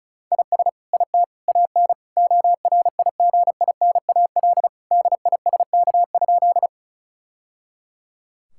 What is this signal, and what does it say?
Morse code, heard as a keyed tone, about 34 wpm, 700 Hz. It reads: IS IT AN ORIGINAL DISK?
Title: Sentences from Top 1000 Words - Review Part 6 - 35wpm